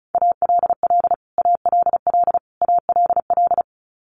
Morse code, heard as a keyed tone, about 34 wpm, 700 Hz. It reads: ALL ALL ALL
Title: Top 500 Words Repeated 35wpm